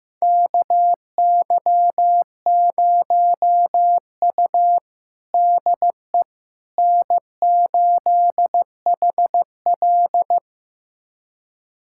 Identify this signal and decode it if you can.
Morse code, heard as a keyed tone, about 15 wpm, 700 Hz. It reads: KY0U DE N8HL
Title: USA Callsign DE Callsign - 4 Characters - 15wpm